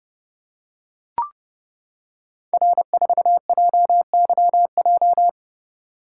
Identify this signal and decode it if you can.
Morse code, heard as a keyed tone, about 30 wpm, 700 Hz. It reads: R4JYJ
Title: Code Groups - 5 Letter:Numbers - 30wpm